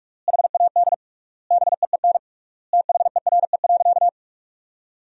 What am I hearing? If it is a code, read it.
Morse code, heard as a keyed tone, about 42 wpm, 700 Hz. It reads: HAD BEEN THERE.
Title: 3x Word-Spacing - Sentences from Top 300 Words - 45wpm